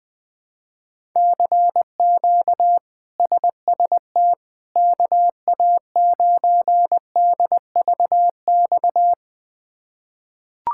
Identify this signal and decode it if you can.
Morse code, heard as a keyed tone, about 20 wpm, 700 Hz. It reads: CQ SST KA9DVX
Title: K1USN SST QSO Practice - 20wpm